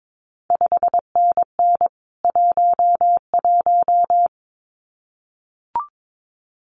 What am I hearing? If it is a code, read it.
Morse code, heard as a keyed tone, about 22 wpm, 700 Hz. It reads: 5NN 11
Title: CQ WW DX Contest Exchange - 22wpm